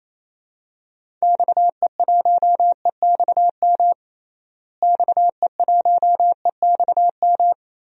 Morse code, heard as a keyed tone, about 28 wpm, 700 Hz. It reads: XE1EXM XE1EXM
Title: DXCC-100 Call Signs - 6 Characters - Repeated 3x - 28wpm